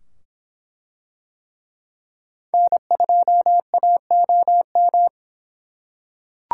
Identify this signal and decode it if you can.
Morse code, heard as a keyed tone, about 24 wpm, 700 Hz. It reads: N2AOM